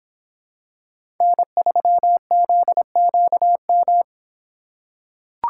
Morse code, heard as a keyed tone, about 26 wpm, 700 Hz. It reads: N3ZQM